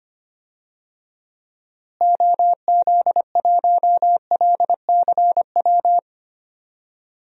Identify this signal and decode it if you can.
Morse code, heard as a keyed tone, about 25 wpm, 700 Hz. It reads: OZ1LCW
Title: DXCC-100 Call Signs - 6 Characters - Repeated 2x - 25wpm